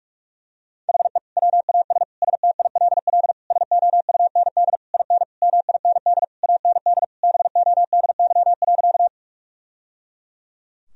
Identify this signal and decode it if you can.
Morse code, heard as a keyed tone, about 46 wpm, 700 Hz. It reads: HE WAS STILL SOUND IN MIND AND BODY.
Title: Sentences from Top 1000 Words - Review Part 2 - 45wpm